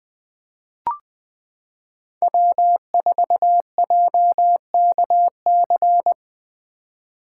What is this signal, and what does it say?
Morse code, heard as a keyed tone, about 20 wpm, 700 Hz. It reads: W4JKC
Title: 2023 SOTA Chaser Call Signs - 20wpm